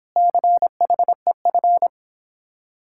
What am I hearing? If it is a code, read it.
Morse code, heard as a keyed tone, about 26 wpm, 700 Hz. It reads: CHEF